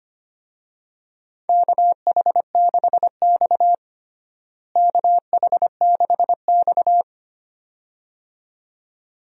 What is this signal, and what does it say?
Morse code, heard as a keyed tone, about 25 wpm, 700 Hz. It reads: KH6X KH6X
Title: DXCC-100 Call Signs - 4 Characters - Repeated 2x - 25wpm